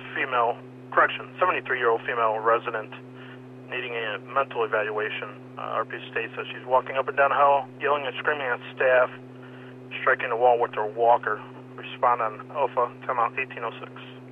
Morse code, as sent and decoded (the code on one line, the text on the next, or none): none